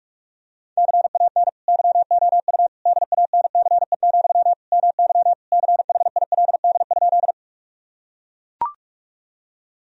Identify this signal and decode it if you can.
Morse code, heard as a keyed tone, about 41 wpm, 700 Hz. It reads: CAN YOU DANCE, MY CHILD?
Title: Sentences from Top 1000 Words - Review Part 4 - 45wpm